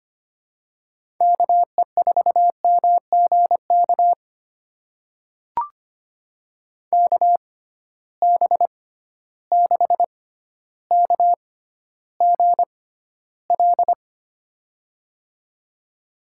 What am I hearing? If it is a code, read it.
Morse code, heard as a keyed tone, about 25 wpm, spaced as for 13 wpm, 700 Hz. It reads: KE4MGK KB6KGL
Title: USA Call Signs - 6 Characters - 6x Character Spacing - 25wpm